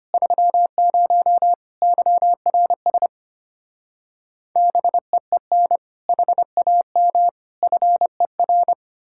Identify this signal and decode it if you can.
Morse code, heard as a keyed tone, about 33 wpm, 700 Hz. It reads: 30 YRS BEEN HAM FER